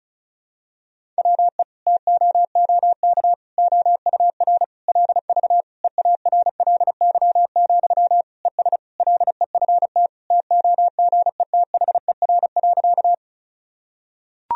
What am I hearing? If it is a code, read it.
Morse code, heard as a keyed tone, about 36 wpm, 700 Hz. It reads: WE TOOK OUR LV EARLY, ES LEFT TOGETHER.
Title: Sentences from Top 400 Words and Abbreviations 35wpm